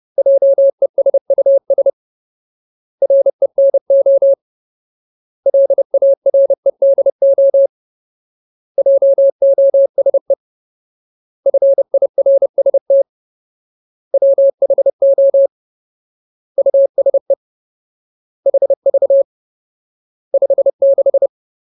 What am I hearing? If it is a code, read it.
Morse code, heard as a keyed tone, about 30 wpm, 550 Hz. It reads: JESUS RENO LAREDO JOSE FIRST WHO USE HV 56
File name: Ultimate Word List / 30WPM Ultimate